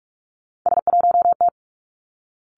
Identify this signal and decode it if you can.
Morse code, heard as a keyed tone, about 46 wpm, 700 Hz. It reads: S1T
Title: Three Challenging Letter-Number Combinations 45wpm